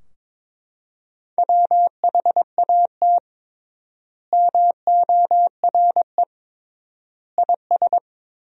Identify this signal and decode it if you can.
Morse code, heard as a keyed tone, about 21 wpm, 700 Hz. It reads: WHAT MORE IS
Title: 3x Word-Spacing - Sentences from Top 500 Words - 22wpm